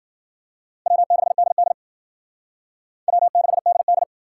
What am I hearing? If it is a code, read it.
Morse code, heard as a keyed tone, about 50 wpm, 700 Hz. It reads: W6DD W6DD
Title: Speed Racing - USA 1x2 Call Signs 40wpm